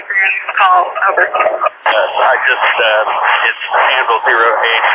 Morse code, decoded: none

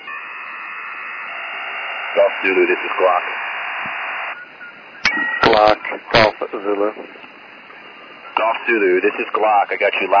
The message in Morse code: none

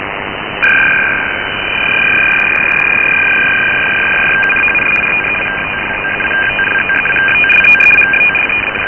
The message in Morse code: none

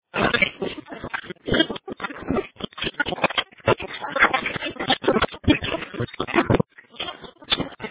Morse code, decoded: none

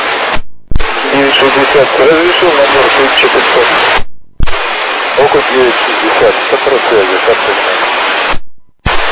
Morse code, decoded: none

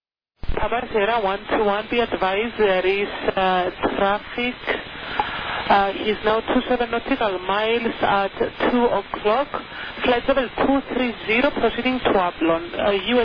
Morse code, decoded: none

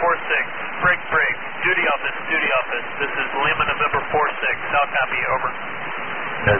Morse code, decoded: none

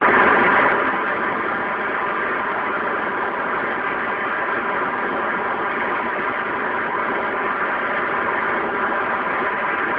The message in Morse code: none

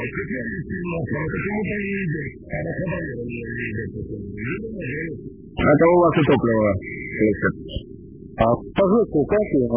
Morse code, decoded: none

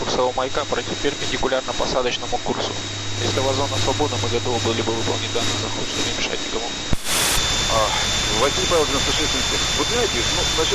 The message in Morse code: none

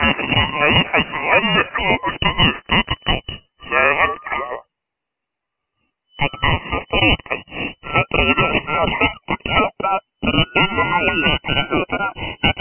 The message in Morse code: none

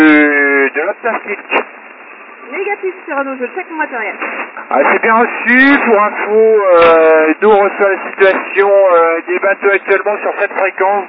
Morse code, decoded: none

